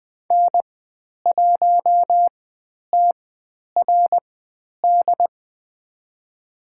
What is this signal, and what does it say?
Morse code, heard as a keyed tone, about 20 wpm, spaced as for 10 wpm, 700 Hz. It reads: N1TRD
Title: 109 Lesson 18g - Letter m - Call Signs 20wpm